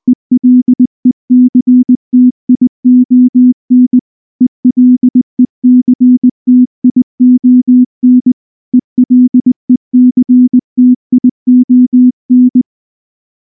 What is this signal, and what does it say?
Morse code, heard as a keyed tone, about 19 wpm, 250 Hz. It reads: ELECTION ELECTION ELECTION
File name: morse / morse-training-20WPM-6